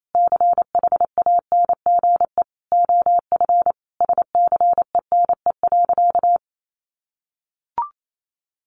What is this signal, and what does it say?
Morse code, heard as a keyed tone, about 28 wpm, 700 Hz. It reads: CHANGE OF SCENE.
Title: Sentences from Top 1100 Words 28wpm